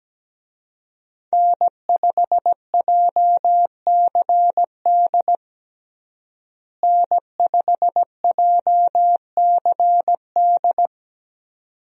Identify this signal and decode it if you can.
Morse code, heard as a keyed tone, about 17 wpm, 700 Hz. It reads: N5JCD N5JCD
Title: USA Call Signs - 5 Characters - Repeated 2x - 17wpm